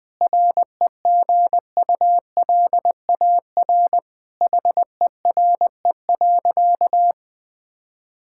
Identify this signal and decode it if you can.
Morse code, heard as a keyed tone, about 20 wpm, 700 Hz. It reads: REGULAR HERE.
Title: Sentences from Top 1000 Words - Review Part 6 - 20wpm